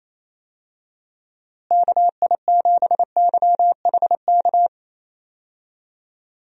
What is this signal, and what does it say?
Morse code, heard as a keyed tone, about 28 wpm, 700 Hz. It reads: KI7YHK
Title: USA Call Signs - 6 Characters - Repeated 3x - 28wpm